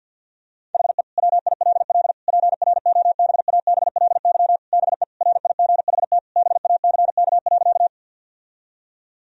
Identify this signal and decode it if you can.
Morse code, heard as a keyed tone, about 50 wpm, 700 Hz. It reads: HE WILL PROBABLY BE RIGHT BACK.
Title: Sentences from Top 1000 Words - Review Part 4 - 50wpm